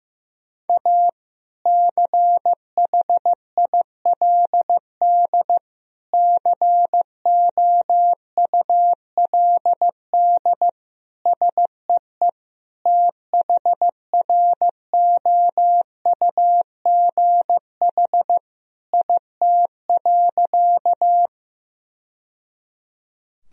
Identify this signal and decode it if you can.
Morse code, heard as a keyed tone, about 16 wpm, 700 Hz. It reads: A CHILD COULD SEE THROUGH IT.
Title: Sentences from Top 1000 Words - Review Part 1 - 15wpm